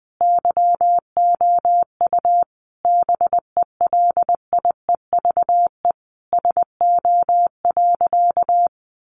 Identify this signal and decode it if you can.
Morse code, heard as a keyed tone, about 20 wpm, 700 Hz. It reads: YOU BELIEVE SO.
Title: Sentences from Top 1000 Words 20wpm